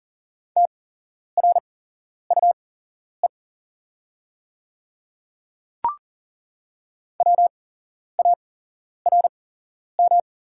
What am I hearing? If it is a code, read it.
Morse code, heard as a keyed tone, about 37 wpm, spaced as for 11 wpm, 700 Hz. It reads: TRUE WARM
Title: 8x Character Spacing - Top 500 Words - 40wpm